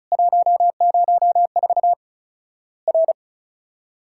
Morse code, heard as a keyed tone, about 36 wpm, 700 Hz. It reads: 104 R
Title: Worked All Europe DX Contest - QTC Exchange - 35wpm